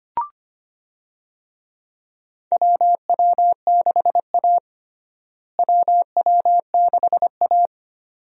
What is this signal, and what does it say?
Morse code, heard as a keyed tone, about 25 wpm, 700 Hz. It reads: WW6A WW6A
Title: USA Call Signs - 4 Characters - Repeated 2x - 25wpm